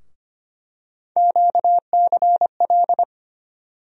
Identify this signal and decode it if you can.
Morse code, heard as a keyed tone, about 25 wpm, 700 Hz. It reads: QCL
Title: USA Call Sign Postfixes - 3 Letters 25wpm